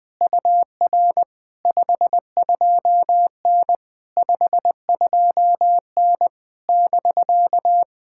Morse code, heard as a keyed tone, about 20 wpm, 700 Hz. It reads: UR 52N 52N <BK>
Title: POTA QSO Practice - 20wpm